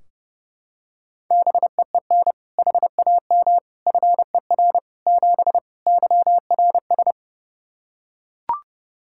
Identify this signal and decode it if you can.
Morse code, heard as a keyed tone, about 30 wpm, 700 Hz. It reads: BEEN HAM FER 7 YRS
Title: K1USN SST QSO Practice 30wpm